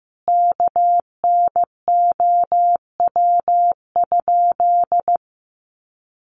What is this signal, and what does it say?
Morse code, heard as a keyed tone, about 15 wpm, 700 Hz. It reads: KNOW?
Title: Sentences from Top 200 Words 15wpm